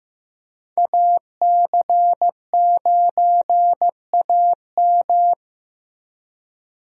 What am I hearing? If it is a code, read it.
Morse code, heard as a keyed tone, about 15 wpm, 700 Hz. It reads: AC9AM
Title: USA Call Signs - 5 Characters - Repeated 2x - 15wpm